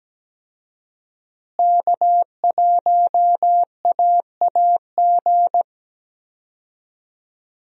Morse code, heard as a keyed tone, about 17 wpm, 700 Hz. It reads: K1AAG
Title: USA Call Signs - 5 Characters - Repeated 3x - 17wpm